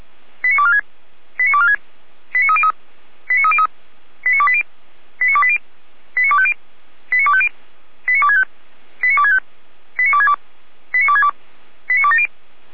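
Five-tone selective call (ZVEI), heard as quick runs of five tones, tones of about 70 ms. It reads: 89246 89246 89303 89303 89280 89280 89250 89250 89265 89265 89262 89262 89270
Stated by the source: ZVEI five-tone call received on SDR.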